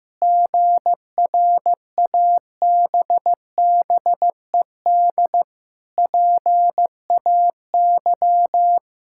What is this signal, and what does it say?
Morse code, heard as a keyed tone, about 15 wpm, 700 Hz. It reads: GRABBED PAY
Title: Pangram Challenge v2 - 15wpm